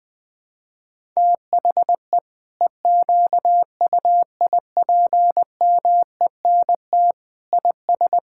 Morse code, heard as a keyed tone, about 20 wpm, 700 Hz. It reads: THE EQUIPMENT IS
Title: Sentences from Top 1000 Words - Review Part 5 - 20wpm